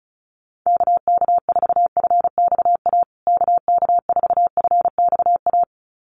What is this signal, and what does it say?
Morse code, heard as a keyed tone, about 34 wpm, 700 Hz. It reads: KK4FXA KK4FXA
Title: USA Call Signs Repeated 2x 35wpm